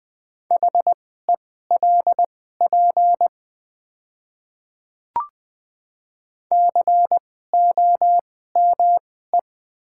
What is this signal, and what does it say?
Morse code, heard as a keyed tone, about 20 wpm, spaced as for 14 wpm, 700 Hz. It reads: HELP COME
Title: 2x Character Spacing - Top 200 Words - 20wpm